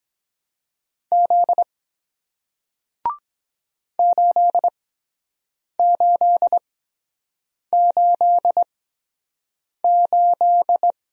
Morse code, heard as a keyed tone, about 23 wpm, spaced as for 7 wpm, 700 Hz. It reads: Z 8888